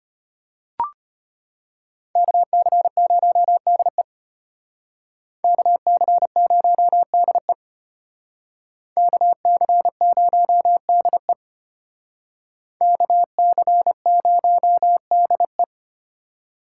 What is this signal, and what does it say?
Morse code, heard as a keyed tone, about 31 wpm, 700 Hz. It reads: KC0DE KC0DE KC0DE KC0DE
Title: Speed Racing - USA 2x2 Call Signs 25wpm